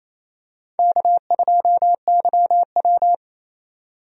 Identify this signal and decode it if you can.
Morse code, heard as a keyed tone, about 28 wpm, 690 Hz. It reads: K2YW
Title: USA Call Signs - 4 Characters - 28wpm